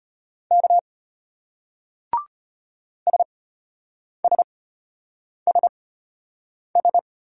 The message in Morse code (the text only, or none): K SSSS